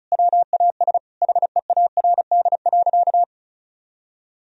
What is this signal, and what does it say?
Morse code, heard as a keyed tone, about 36 wpm, 700 Hz. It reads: WAS HEARD.
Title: Sentences from Top 400 Words and Abbreviations 35wpm